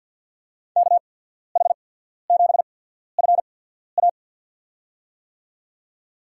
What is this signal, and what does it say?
Morse code, heard as a keyed tone, about 54 wpm, spaced as for 14 wpm, 700 Hz. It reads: KH7FA
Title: USA Call Signs - 5 Characters - 8x Character Spacing - 50wpm